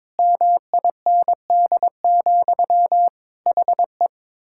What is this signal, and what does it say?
Morse code, heard as a keyed tone, about 22 wpm, 700 Hz. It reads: MIND, HE
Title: Sentences from Top 1000 Words - Review Part 6 - 22wpm